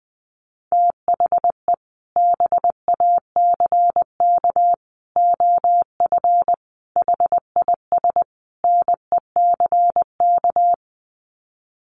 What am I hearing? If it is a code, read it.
Morse code, heard as a keyed tone, about 20 wpm, 700 Hz. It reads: THE BACK OF HIS NECK
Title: Sets of 5 Words Encore 20wpm